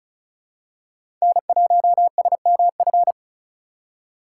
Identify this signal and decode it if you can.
Morse code, heard as a keyed tone, about 34 wpm, 700 Hz. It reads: N1SMF
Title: USA Call Signs - 6x Character Spacing - 35wpm